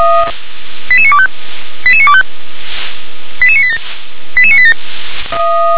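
Five-tone selective call (ZVEI), heard as quick runs of five tones, tones of about 70 ms. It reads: 88925 88925 88987 88987
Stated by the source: ZVEI five-tone selective call received on an SDR.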